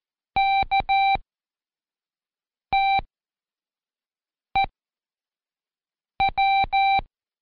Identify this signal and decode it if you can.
Morse code, heard as a keyed tone, about 14 wpm, spaced as for 5 wpm, 780 Hz. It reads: KTEW
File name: Code Quick / AudioDownld1 / Seg18-Day2-Quiz ASYO